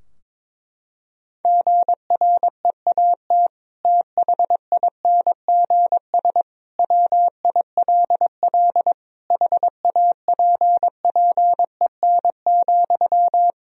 Morse code, heard as a keyed tone, about 22 wpm, 700 Hz. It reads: GREAT THINGS WILL HAPPEN,
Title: Sentences from Top 1000 Words - Review Part 4 - 22wpm